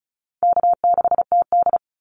Morse code, heard as a keyed tone, about 34 wpm, 700 Hz. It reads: K6TD
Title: Easy Call Signs Repeated 2x - 35wpm